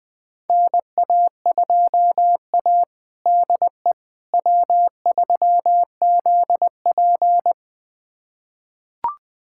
Morse code, heard as a keyed tone, about 21 wpm, 700 Hz. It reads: NA2A DE W3ZP
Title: USA Callsign DE Callsign - 4 Characters - 20wpm